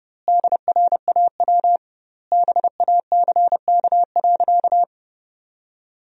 Morse code, heard as a keyed tone, about 32 wpm, 700 Hz. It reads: DRAW BACK.
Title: 2x Word-Spacing - Sentences from Top 300 Words - 30wpm